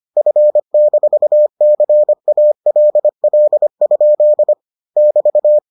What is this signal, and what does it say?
Morse code, heard as a keyed tone, about 25 wpm, 600 Hz. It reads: F-CALL? <BT>